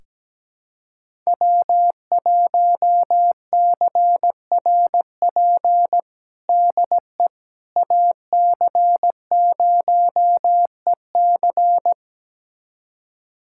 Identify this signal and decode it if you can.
Morse code, heard as a keyed tone, about 17 wpm, 700 Hz. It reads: W1CRP DE AC0EC
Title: USA Callsign DE Callsign - 5 Characters - 17wpm